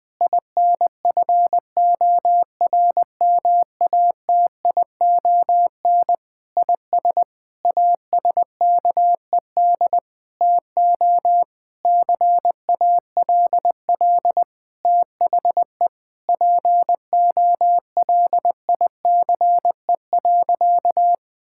Morse code, heard as a keyed tone, about 20 wpm, 700 Hz. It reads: INFORMATION IS ASKED TO CALL THE POLICE.
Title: Sentences with 7-Word Phrases - Spoken Beginning - 20wpm